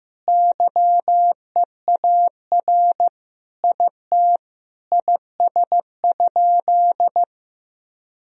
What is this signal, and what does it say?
Morse code, heard as a keyed tone, about 15 wpm, 700 Hz. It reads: YEAR IT IS?
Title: Sentences from Top 1000 Words - Review Part 1 - 15wpm